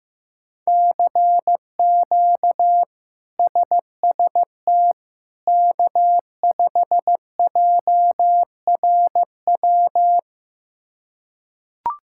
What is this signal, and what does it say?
Morse code, heard as a keyed tone, about 15 wpm, 700 Hz. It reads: CQ SST K5JRW
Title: K1USN SST QSO Practice - 15wpm